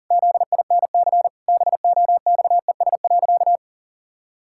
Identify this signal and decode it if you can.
Morse code, heard as a keyed tone, about 40 wpm, 700 Hz. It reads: ZINC BOXES.
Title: Pangram Challenge v2 - 40wpm